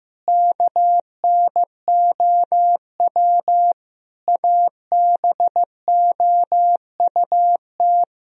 Morse code, heard as a keyed tone, about 16 wpm, 700 Hz. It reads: KNOW ABOUT
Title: Sentences from Top 1000 Words - Review Part 3 - 15wpm